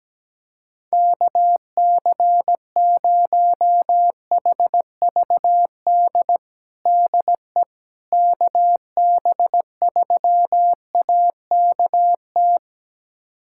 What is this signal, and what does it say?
Morse code, heard as a keyed tone, about 17 wpm, 700 Hz. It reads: KC0HVD DE KB3AKT
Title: USA Callsign DE Callsign 17wpm